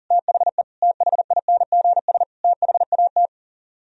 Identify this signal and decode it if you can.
Morse code, heard as a keyed tone, about 49 wpm, 700 Hz. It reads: THE THINGS THAT
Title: Speed-Racing - Sets of 4 Words 40wpm